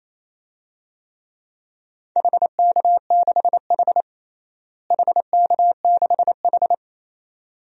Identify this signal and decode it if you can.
Morse code, heard as a keyed tone, about 28 wpm, 700 Hz. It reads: HK6H HK6H
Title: DXCC-100 Call Signs - 4 Characters - Repeated 2x - 28wpm